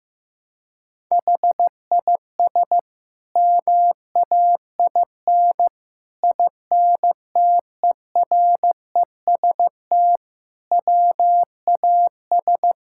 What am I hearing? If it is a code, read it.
Morse code, heard as a keyed tone, about 16 wpm, 700 Hz. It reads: HIS MAIN INTEREST WAS
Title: Sentences from Top 1000 Words - Review Part 6 - 15wpm